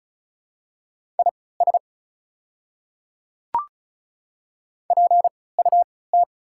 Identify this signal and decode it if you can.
Morse code, heard as a keyed tone, about 36 wpm, spaced as for 20 wpm, 700 Hz. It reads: IS PUT
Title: 3x Character Spacing - Top 300 Words - 35wpm